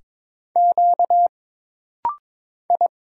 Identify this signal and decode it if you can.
Morse code, heard as a keyed tone, about 21 wpm, 700 Hz. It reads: Q I
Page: This rapid-fire timing is geared to push your ICR (Instant Character Recognition) skills.